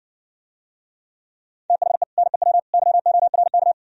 Morse code, heard as a keyed tone, about 68 wpm, 700 Hz. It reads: THE NEW YORK